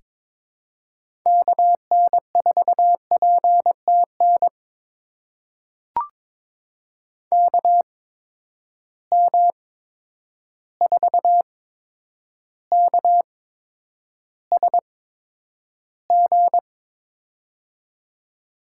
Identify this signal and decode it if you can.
Morse code, heard as a keyed tone, about 22 wpm, spaced as for 9 wpm, 700 Hz. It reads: KN4PTN KM4KSG
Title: USA Call Signs - 8x Character Spacing - 22wpm